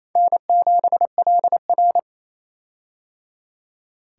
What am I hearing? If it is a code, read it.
Morse code, heard as a keyed tone, about 28 wpm, 700 Hz. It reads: N7LR
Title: K1USN SST QSO Practice - 28wpm